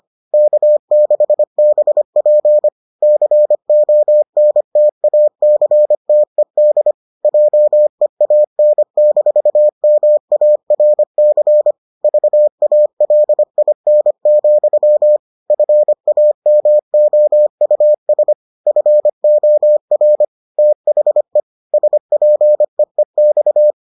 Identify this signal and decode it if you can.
Morse code, heard as a keyed tone, about 25 wpm, 600 Hz. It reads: K6BP CONTACTED JEAN-MARC VALIN, FAMOUS FOR THE SPEEX